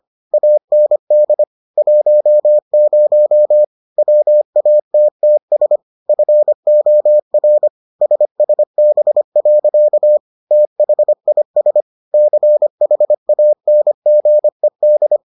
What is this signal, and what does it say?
Morse code, heard as a keyed tone, about 25 wpm, 600 Hz. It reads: AND 10 WATTS FOR SSB. THIS CHANGED